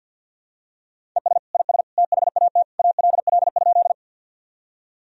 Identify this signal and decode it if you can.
Morse code, heard as a keyed tone, about 52 wpm, 700 Hz. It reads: ES IS THAT ALL?
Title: Sentences from Top 100 Words and Abbreviations 50wpm